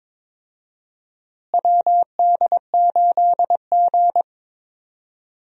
Morse code, heard as a keyed tone, about 22 wpm, 700 Hz. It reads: WD8G
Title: USA Call Signs - 4 Characters - Repeated 2x - 22wpm